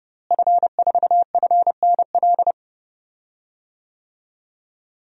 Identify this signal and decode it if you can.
Morse code, heard as a keyed tone, about 30 wpm, 700 Hz. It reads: F4FNL
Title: DXCC-100 Call Signs - 5 Characters - Repeated 3x - 30wpm